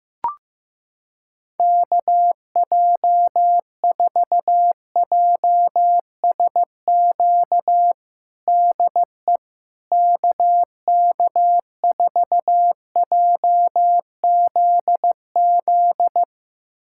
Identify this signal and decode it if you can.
Morse code, heard as a keyed tone, about 15 wpm, 700 Hz. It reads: KJ4JSQ DE KK4JZZ
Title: USA Callsign DE Callsign 15wpm